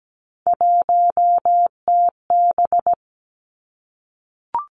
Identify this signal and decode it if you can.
Morse code, heard as a keyed tone, about 17 wpm, 700 Hz. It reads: 1TB
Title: Three Challenging Letter-Number Combinations 17wpm